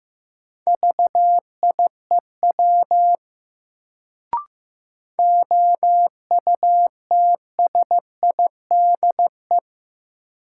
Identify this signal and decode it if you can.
Morse code, heard as a keyed tone, about 15 wpm, 700 Hz. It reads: VIEW OUTSIDE
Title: Sentences from Top 900 Words - New Word Drill - 15wpm